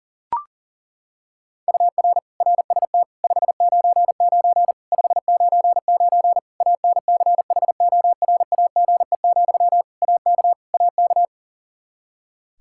Morse code, heard as a keyed tone, about 40 wpm, 700 Hz. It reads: UR RST 599 599 ANCHORAGE, AK AK
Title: Practice QSO - Exchange 1 - Beginning - 40wpm